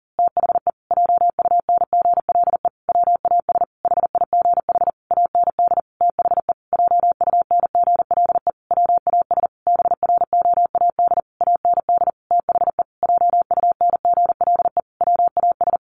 Morse code, heard as a keyed tone, about 40 wpm, 700 Hz. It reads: THE JUNGLE WAS HIGH AND THE JUNGLE WAS BROAD AND THE JUNGLE WAS
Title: A Sound Of Thunder 40wpm